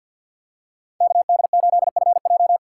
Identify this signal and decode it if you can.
Morse code, heard as a keyed tone, about 50 wpm, 700 Hz. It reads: KD8FJ
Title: USA Call Signs - 5 Characters - 6x Character Spacing - 50wpm